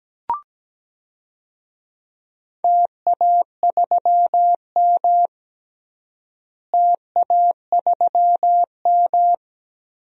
Morse code, heard as a keyed tone, about 17 wpm, 700 Hz. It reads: TA3M TA3M
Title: DXCC-100 Call Signs - 4 Characters - Repeated 3x - 17wpm